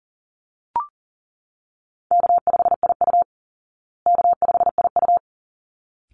Morse code, heard as a keyed tone, about 40 wpm, 700 Hz. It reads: K5IU K5IU
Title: Easy Call Signs Repeated 2x - 40wpm